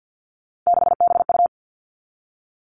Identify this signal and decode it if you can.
Morse code, heard as a keyed tone, about 50 wpm, 700 Hz. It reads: 6BU
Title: All Cut Numbers - 3-Digit 50wpm